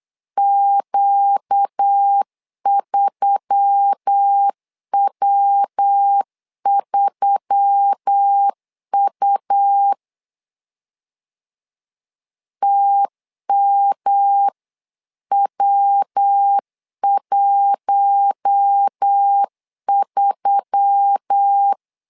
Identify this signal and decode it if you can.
Morse code, heard as a keyed tone, about 8 wpm, 800 Hz. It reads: Q3W3U TMW13
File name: LEZIONI / 4 Q1S3 8wpm 3037